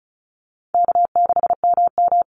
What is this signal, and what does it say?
Morse code, heard as a keyed tone, about 34 wpm, 700 Hz. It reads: K6MM
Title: Easy Call Signs Repeated 3x - 35wpm